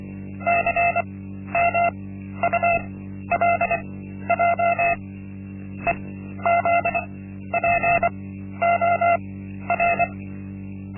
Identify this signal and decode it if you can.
Morse code, heard as a keyed tone, about 24 wpm, spaced as for 12 wpm, 690 Hz. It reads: CMULJ EZPOR
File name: Linksmakalnis-1